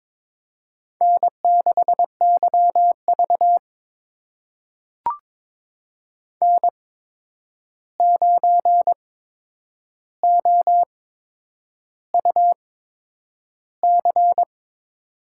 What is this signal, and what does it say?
Morse code, heard as a keyed tone, about 22 wpm, spaced as for 9 wpm, 700 Hz. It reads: N6YV N9OUC
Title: USA Call Signs - 8x Character Spacing - 22wpm